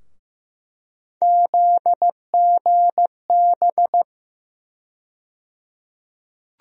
Morse code, heard as a keyed tone, about 15 wpm, 700 Hz. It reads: ZGB
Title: USA Call Sign Postfixes - 3 Letters 15wpm